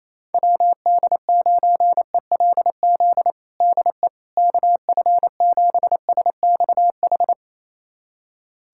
Morse code, heard as a keyed tone, about 28 wpm, 700 Hz. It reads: WD9ELZ DE KF7SXH
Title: USA Callsign DE Callsign - 6 Characters - 28wpm